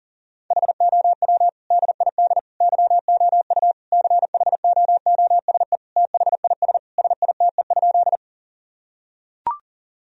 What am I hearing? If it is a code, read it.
Morse code, heard as a keyed tone, about 40 wpm, 700 Hz. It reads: HOW DID YOU CHOOSE THIS SITE?
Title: Sentences from Top 1000 Words - Review Part 4 - 40wpm